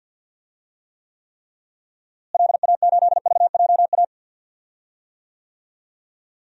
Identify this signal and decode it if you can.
Morse code, heard as a keyed tone, about 46 wpm, 700 Hz. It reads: LA8VJA
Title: DXCC-100 Call Signs - 6 Characters - Repeated 3x - 50wpm